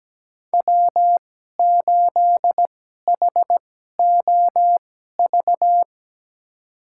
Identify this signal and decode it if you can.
Morse code, heard as a keyed tone, about 17 wpm, spaced as for 12 wpm, 700 Hz. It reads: W8HOV
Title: USA Call Signs - 5 Characters - 2x Character Spacing - 17wpm